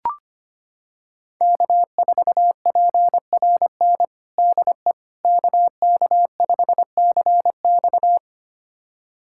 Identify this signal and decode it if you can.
Morse code, heard as a keyed tone, about 25 wpm, 700 Hz. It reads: K4PRN DE KK5CX
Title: USA Callsign DE Callsign - 5 Characters - 25wpm